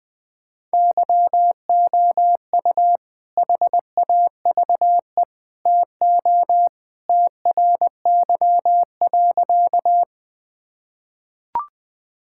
Morse code, heard as a keyed tone, about 20 wpm, 700 Hz. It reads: YOU HAVE TO TRY.
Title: Sentences with 3-Word Phrases - Spoken Beginning - 20wpm